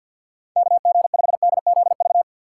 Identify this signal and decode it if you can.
Morse code, heard as a keyed tone, about 52 wpm, 700 Hz. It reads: KG5DZV
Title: USA Call Signs - 6 Characters - 8x Character Spacing - 50wpm